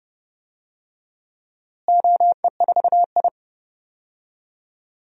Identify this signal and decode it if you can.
Morse code, heard as a keyed tone, about 30 wpm, 700 Hz. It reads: OE4I
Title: DXCC-100 Call Signs - 4 Characters - Repeated 3x - 30wpm